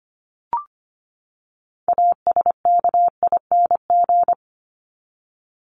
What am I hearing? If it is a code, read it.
Morse code, heard as a keyed tone, about 25 wpm, 700 Hz. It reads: ASKING
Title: ING Suffix - Top 100 Words with Base Words 25wpm